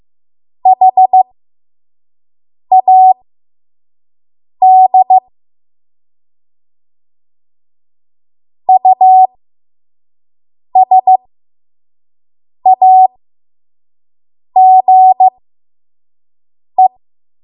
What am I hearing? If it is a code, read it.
Morse code, heard as a keyed tone, about 15 wpm, spaced as for 5 wpm, 750 Hz. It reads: HAD USAGE
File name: Morse / 05WPM